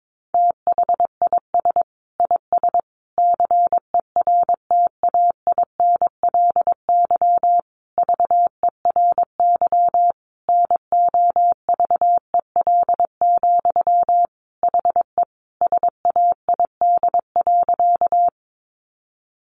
Morse code, heard as a keyed tone, about 22 wpm, 700 Hz. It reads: THIS IS CERTAINLY VERY NOVEL, HE SAID.
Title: Sentences from Top 1300 Words 22wpm